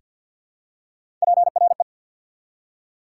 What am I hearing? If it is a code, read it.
Morse code, heard as a keyed tone, about 54 wpm, 690 Hz. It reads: PRE
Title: Prefixes and Suffixes 50wpm